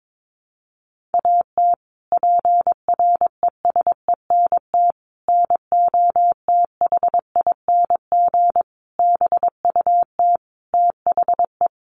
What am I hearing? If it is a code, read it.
Morse code, heard as a keyed tone, about 22 wpm, 700 Hz. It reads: AT PRESENT NOTHING BUT THE
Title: Sentences from Top 1500 Words 22wpm